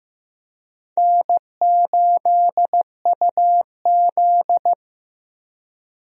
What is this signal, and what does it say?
Morse code, heard as a keyed tone, about 15 wpm, 700 Hz. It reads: N8UZ
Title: USA Call Signs - 4 Characters - Repeated 3x - 15wpm